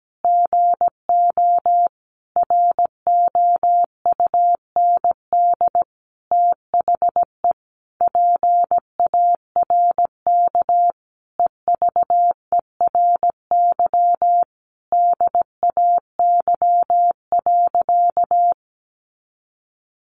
Morse code, heard as a keyed tone, about 17 wpm, 700 Hz. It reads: GO ROUND THE PARK EVERY DAY.
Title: Sentences from Top 1300 Words 17wpm